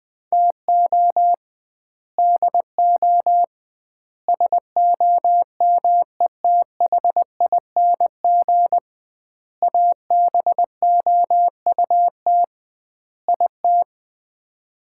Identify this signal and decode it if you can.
Morse code, heard as a keyed tone, about 21 wpm, 700 Hz. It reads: TO DO SOMETHING ABOUT IT
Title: 2x Word-Spacing - Sets of 5 Words - 20wpm